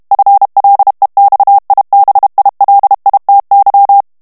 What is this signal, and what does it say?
Morse code, heard as a keyed tone, about 32 wpm, 800 Hz. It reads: FLEXIBILITY
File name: morse mp3 / ANGLAIS texte clair 3 vitesse 30 mots minute